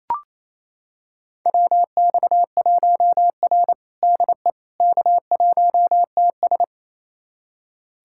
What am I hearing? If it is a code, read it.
Morse code, heard as a keyed tone, about 27 wpm, 700 Hz. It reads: WX1R DE K1TS
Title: USA Callsign DE Callsign - 4 Characters - 28wpm